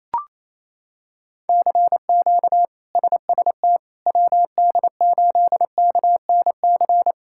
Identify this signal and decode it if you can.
Morse code, heard as a keyed tone, about 28 wpm, 700 Hz. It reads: CQ SST WD8KNC
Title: K1USN SST QSO Practice - 28wpm